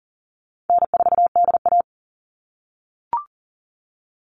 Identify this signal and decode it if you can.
Morse code, heard as a keyed tone, about 40 wpm, 700 Hz. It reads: N4DA
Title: NA SOTA Chaser Callsigns 40wpm